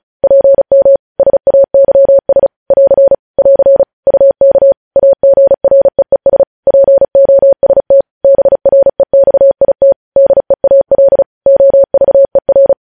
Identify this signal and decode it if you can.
Morse code, heard as a keyed tone, about 36 wpm, 550 Hz. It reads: PM SAYS <AR> <AR> UK AGREES POST BREXIT DEAL OVER
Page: news headlines in morse code at 35 wpm